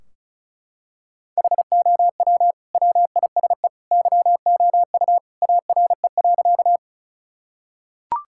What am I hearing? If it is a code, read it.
Morse code, heard as a keyed tone, about 36 wpm, 700 Hz. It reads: HOW WISE YOU ARE.
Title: Sentences from Top 1600 Words 35wpm